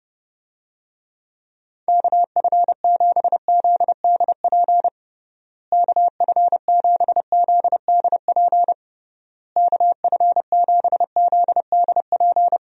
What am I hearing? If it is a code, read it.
Morse code, heard as a keyed tone, about 32 wpm, 700 Hz. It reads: KF7ZDP KF7ZDP KF7ZDP
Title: USA Call Signs - 6 Characters - Repeated 3x - 30wpm